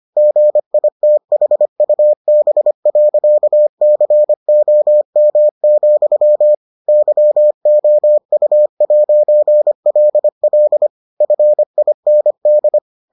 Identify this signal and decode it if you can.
Morse code, heard as a keyed tone, about 25 wpm, 600 Hz. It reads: GITHUB.COM, YOU'LL FIND